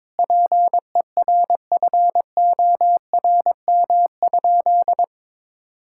Morse code, heard as a keyed tone, about 21 wpm, 700 Hz. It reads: PERFORM?
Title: Sentences from Top 1000 Words - Review Part 6 - 22wpm